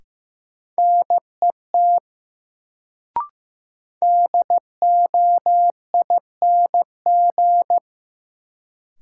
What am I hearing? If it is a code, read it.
Morse code, heard as a keyed tone, about 15 wpm, 700 Hz. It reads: NET DOING
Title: Sentences from Top 1400 Words - New Word Drill - 15wpm